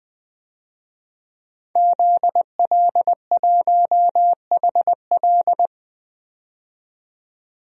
Morse code, heard as a keyed tone, about 20 wpm, 700 Hz. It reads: ZL1HL
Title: DXCC-100 Call Signs - 5 Characters - 20wpm